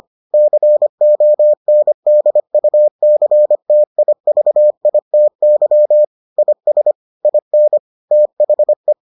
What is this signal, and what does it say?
Morse code, heard as a keyed tone, about 25 wpm, 600 Hz. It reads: CONDUCTIVITY IS IN THE